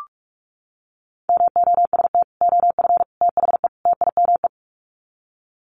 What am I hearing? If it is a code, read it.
Morse code, heard as a keyed tone, about 44 wpm, 700 Hz. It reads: MOST OF THE TIME
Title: Sets of 4 Words 45wpm